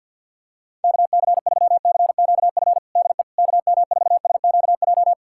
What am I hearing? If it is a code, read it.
Morse code, heard as a keyed tone, about 51 wpm, 700 Hz. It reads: KC3CQF DE KM4SQJ
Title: USA Callsign DE Callsign - 6 Characters - 50wpm